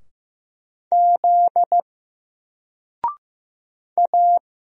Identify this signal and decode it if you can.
Morse code, heard as a keyed tone, about 15 wpm, 700 Hz. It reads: Z A
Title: Single Letter-Number 15wpm